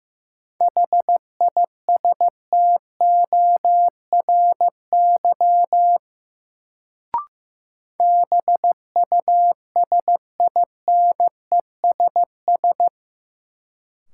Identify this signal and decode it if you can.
Morse code, heard as a keyed tone, about 15 wpm, 700 Hz. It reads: HISTORY BUSINESS
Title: Sentences from Top 600 Words - New Word Drill - 15wpm